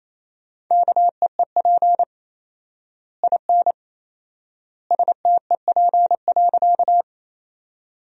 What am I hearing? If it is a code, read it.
Morse code, heard as a keyed tone, about 28 wpm, 700 Hz. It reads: KEEP IN STEP.
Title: 4x Word-Spacing - Sentences from Top 400 Words - 28wpm